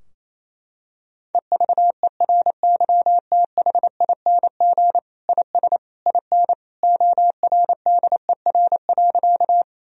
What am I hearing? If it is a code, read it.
Morse code, heard as a keyed tone, about 28 wpm, 700 Hz. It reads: EVERYTHING IS IN ORDER.
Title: Sentences from Top 1000 Words - Review Part 2 - 28wpm